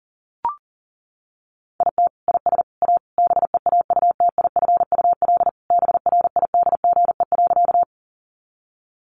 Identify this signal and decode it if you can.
Morse code, heard as a keyed tone, about 40 wpm, 700 Hz. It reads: IT IS A BEAUTIFUL BRIDGE.
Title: Sentences from Top 1200 Words 40wpm